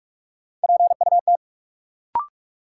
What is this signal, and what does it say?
Morse code, heard as a keyed tone, about 42 wpm, 700 Hz. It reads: PUT
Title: Sentences from Top 200 Words - New Word Drill - 45wpm